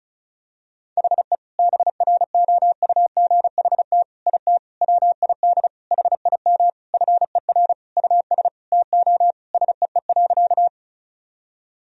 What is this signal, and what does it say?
Morse code, heard as a keyed tone, about 36 wpm, 700 Hz. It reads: HE BROUGHT IT WID HIM FER US TO SEE.
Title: Sentences from Top 500 Words and Abbreviations 35wpm